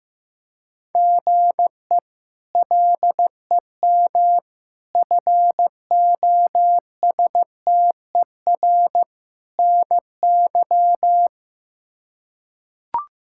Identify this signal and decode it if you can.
Morse code, heard as a keyed tone, about 15 wpm, 700 Hz. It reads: GE LEM FOSTER NY
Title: K1USN SST QSO Practice - 15wpm